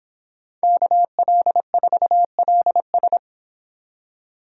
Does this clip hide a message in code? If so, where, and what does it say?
Yes: Morse code from 0.6 to 3.2 s, heard as a keyed tone, about 26 wpm, 700 Hz. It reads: KL4LS